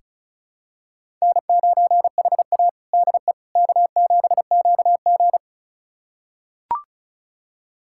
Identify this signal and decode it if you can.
Morse code, heard as a keyed tone, about 36 wpm, 700 Hz. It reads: N9HA DE K7QG
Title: USA Callsign DE Callsign - 4 Characters - 35wpm